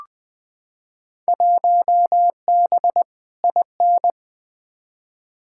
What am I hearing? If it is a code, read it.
Morse code, heard as a keyed tone, about 21 wpm, 700 Hz. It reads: 1B IN
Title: Field Day Exchanges - 20wpm